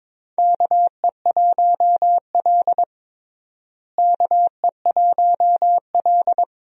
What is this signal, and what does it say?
Morse code, heard as a keyed tone, about 22 wpm, 700 Hz. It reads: KE1L KE1L
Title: USA Call Signs - 4 Characters - Repeated 2x - 22wpm